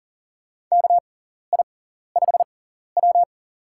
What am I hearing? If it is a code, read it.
Morse code, heard as a keyed tone, about 40 wpm, spaced as for 14 wpm, 690 Hz. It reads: KI5W
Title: USA Call Signs - 4 Characters - 6x Character Spacing - 40wpm